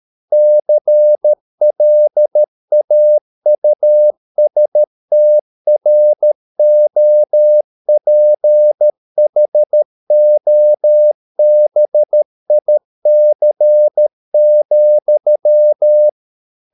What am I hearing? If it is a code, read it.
Morse code, heard as a keyed tone, about 13 wpm, 600 Hz. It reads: CLAUSTROPHOBIC,